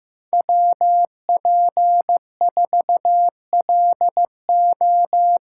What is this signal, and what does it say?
Morse code, heard as a keyed tone, about 15 wpm, 700 Hz. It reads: WP4LO
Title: USA Call Signs - 6 Characters - 8x Character Spacing - 15wpm